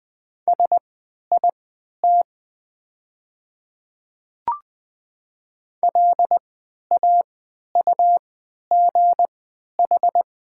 Morse code, heard as a keyed tone, about 20 wpm, spaced as for 11 wpm, 700 Hz. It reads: SIT LAUGH
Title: 3x Character Spacing - Top 500 Words - 20wpm